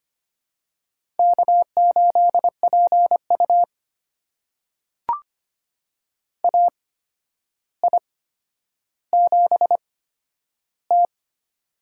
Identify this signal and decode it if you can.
Morse code, heard as a keyed tone, about 25 wpm, spaced as for 11 wpm, 700 Hz. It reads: K8PU AI7T
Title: USA Call Signs - 4 Characters - 8x Character Spacing - 25wpm